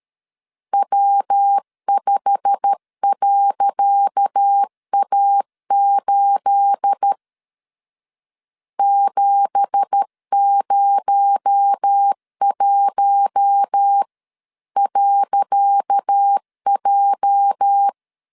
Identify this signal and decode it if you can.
Morse code, heard as a keyed tone, about 13 wpm, 800 Hz. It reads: W5.A8 701.J
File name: LEZIONI / 10 SIGNS / 10C DOT 12wpm 2451